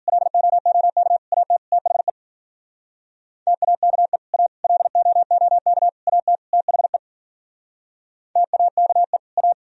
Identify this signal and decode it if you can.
Morse code, heard as a keyed tone, about 52 wpm, 700 Hz. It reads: LOOK AT THE TAKE A LOOK AT THE TAKE A